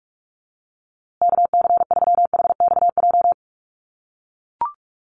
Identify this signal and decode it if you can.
Morse code, heard as a keyed tone, about 44 wpm, 700 Hz. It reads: KC3HXJ
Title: USA Call Signs Repeated 3x 45wpm